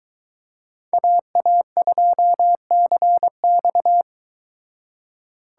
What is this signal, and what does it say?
Morse code, heard as a keyed tone, about 22 wpm, 700 Hz. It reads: AA2CX